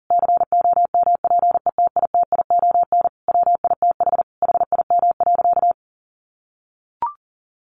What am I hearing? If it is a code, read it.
Morse code, heard as a keyed tone, about 37 wpm, 700 Hz. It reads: COMPETITION WITH HIM.
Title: Sentences from Top 1000 Words 40wpm